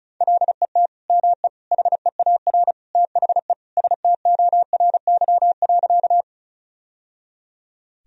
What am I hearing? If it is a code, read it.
Morse code, heard as a keyed tone, about 34 wpm, 700 Hz. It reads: LET ME HEAR THE STORY.
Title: Sentences from Top 1000 Words - Review Part 1 - 35wpm